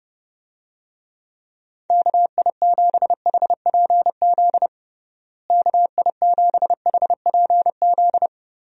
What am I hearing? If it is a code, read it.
Morse code, heard as a keyed tone, about 29 wpm, 700 Hz. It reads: KI7HPZ KI7HPZ
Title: USA Call Signs - 6 Characters - Repeated 2x - 30wpm